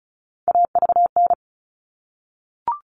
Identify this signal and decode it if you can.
Morse code, heard as a keyed tone, about 34 wpm, 700 Hz. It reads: AVN
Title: All Cut Numbers - 3-Digit 35wpm